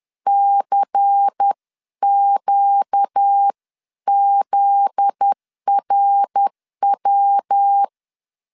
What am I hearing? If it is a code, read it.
Morse code, heard as a keyed tone, about 10 wpm, 800 Hz. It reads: CQZRW
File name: LEZIONI / 7 GF9Z / 7 GF9Z 10wpm 2611